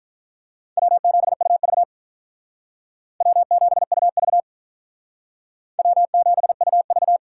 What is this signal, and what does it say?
Morse code, heard as a keyed tone, about 48 wpm, 700 Hz. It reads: W7UV W7UV W7UV